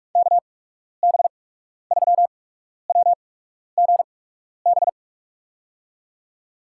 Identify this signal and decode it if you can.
Morse code, heard as a keyed tone, about 46 wpm, spaced as for 12 wpm, 700 Hz. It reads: KB3WGB
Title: USA Call Signs - 8x Character Spacing - 45wpm